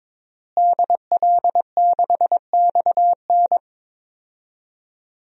Morse code, heard as a keyed tone, about 22 wpm, 700 Hz. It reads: DL6XN
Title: DXCC-100 Call Signs - 5 Characters - Repeated 3x - 22wpm